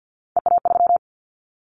Morse code, heard as a keyed tone, about 50 wpm, 700 Hz. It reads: EA3
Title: Callsign Prefixes for Top 25 DXCC - 50wpm